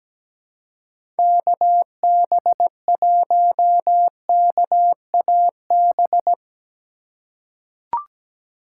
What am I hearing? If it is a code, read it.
Morse code, heard as a keyed tone, about 17 wpm, 700 Hz. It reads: KB1KAB
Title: USA Call Signs - 6 Characters - 4x Character Spacing - 17wpm